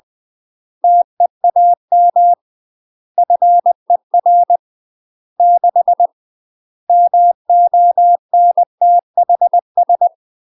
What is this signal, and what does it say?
Morse code, heard as a keyed tone, about 20 wpm, 700 Hz. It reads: TEAM FER 6 MONTHS
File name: UK MASTER QSOs TRAINING / 20WPM / UK MASTER QSOs 0028